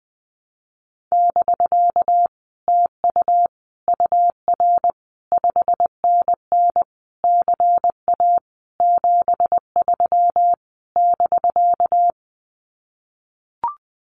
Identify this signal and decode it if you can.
Morse code, heard as a keyed tone, about 19 wpm, 700 Hz. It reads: <BK> TU UR 5NN CA 73 <BK>
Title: Common QSO Phrases Encore 20wpm